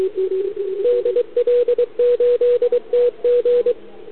Morse code, heard as a keyed tone, about 23 wpm, 490 Hz, with radio noise. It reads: DL8TG
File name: CQWWCW2021 / ZF5T